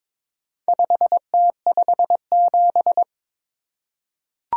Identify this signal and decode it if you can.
Morse code, heard as a keyed tone, about 23 wpm, 700 Hz. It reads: 5T57
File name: practice / 22wpm / Four Letter-Number Combinations - 22wpm